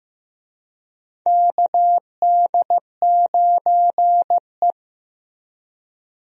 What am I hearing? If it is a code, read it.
Morse code, heard as a keyed tone, about 15 wpm, 700 Hz. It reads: KD9E
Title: USA Call Signs - 4 Characters - Repeated 3x - 15wpm